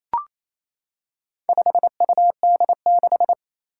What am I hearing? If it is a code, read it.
Morse code, heard as a keyed tone, about 27 wpm, 700 Hz. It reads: 5UD6
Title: Four Challenging Letter-Number Combinations 28wpm